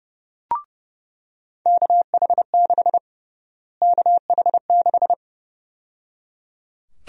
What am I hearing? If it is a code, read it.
Morse code, heard as a keyed tone, about 30 wpm, 700 Hz. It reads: KH6 KH6
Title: USA Call Sign Prefixes - Repeated 2x - 30wpm